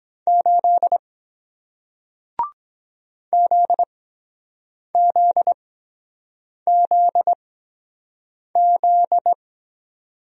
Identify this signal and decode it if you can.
Morse code, heard as a keyed tone, about 23 wpm, spaced as for 7 wpm, 700 Hz. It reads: 8 ZZZZ